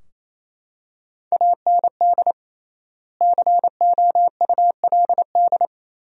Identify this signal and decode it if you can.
Morse code, heard as a keyed tone, about 28 wpm, 700 Hz. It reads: AND COULD
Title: 3x Word-Spacing - Sentences from Top 100 Words - 28wpm